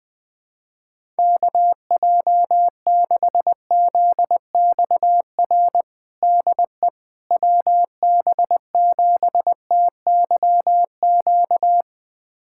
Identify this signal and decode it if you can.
Morse code, heard as a keyed tone, about 20 wpm, 700 Hz. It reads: KJ6ZXR DE WB7TYQ
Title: USA Callsign DE Callsign - 6 Characters - 20wpm